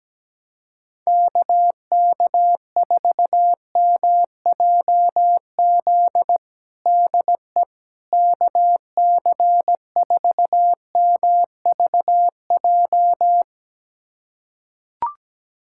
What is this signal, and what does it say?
Morse code, heard as a keyed tone, about 17 wpm, 700 Hz. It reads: KK4MJZ DE KC4MVJ
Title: USA Callsign DE Callsign - 6 Characters - 17wpm